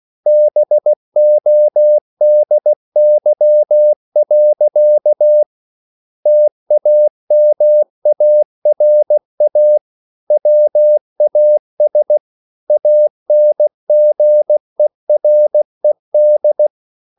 Morse code, heard as a keyed tone, about 16 wpm, 600 Hz. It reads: BODY. TAMARA WAS ANGERED